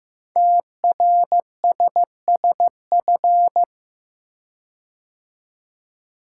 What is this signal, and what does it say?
Morse code, heard as a keyed tone, about 15 wpm, 700 Hz. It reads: TRSSF
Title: Code Groups - 5 Letters - 15wpm